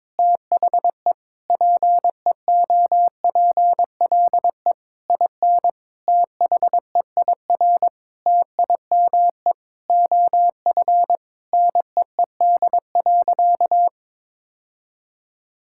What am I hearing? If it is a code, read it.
Morse code, heard as a keyed tone, about 22 wpm, 700 Hz. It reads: THE PEOPLE IN THEIR TIME OF NEED.
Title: Sentences from Top 1000 Words - Review Part 5 - 22wpm